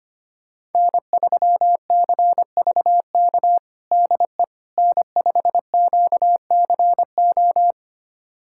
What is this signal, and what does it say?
Morse code, heard as a keyed tone, about 25 wpm, 700 Hz. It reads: N3CVK DE N5QCO
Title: USA Callsign DE Callsign - 5 Characters - 25wpm